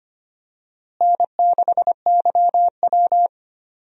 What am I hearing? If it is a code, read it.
Morse code, heard as a keyed tone, about 25 wpm, 700 Hz. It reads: N6YW
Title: USA Call Signs - 4 Characters - Repeated 2x - 25wpm